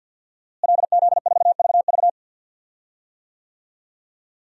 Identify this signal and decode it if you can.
Morse code, heard as a keyed tone, about 47 wpm, 700 Hz. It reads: LZ4VV
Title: DXCC-100 Call Signs - 5 Characters - Repeated 2x - 50wpm